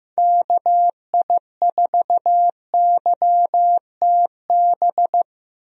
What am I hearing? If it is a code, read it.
Morse code, heard as a keyed tone, about 16 wpm, 700 Hz. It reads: KI4YTB
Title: USA Call Signs - 6 Characters - Repeated 2x - 15wpm